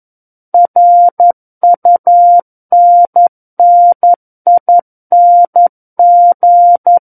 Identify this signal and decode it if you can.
Morse code, heard as a keyed tone, about 11 wpm, 700 Hz. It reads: RUNNING
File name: Book One / 02-Chapter 2 - 11 WPM